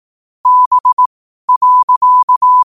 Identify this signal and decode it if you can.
Morse code, heard as a keyed tone, about 18 wpm, spaced as for 12 wpm, 1000 Hz. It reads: B.
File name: qotdfiles / 12wpm